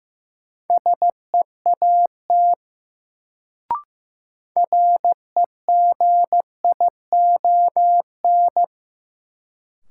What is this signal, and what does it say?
Morse code, heard as a keyed tone, about 15 wpm, 700 Hz. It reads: SEAT REGION
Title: Sentences from Top 1500 Words - New Word Drill Alt - 15wpm